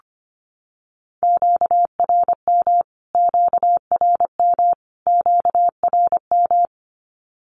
Morse code, heard as a keyed tone, about 25 wpm, 700 Hz. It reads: QRM QRM QRM
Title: Q-Codes Repeated 3x 25wpm